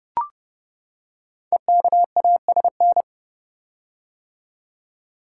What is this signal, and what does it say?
Morse code, heard as a keyed tone, about 28 wpm, 700 Hz. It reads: EKASN
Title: Code Groups - 5 Letters - 30wpm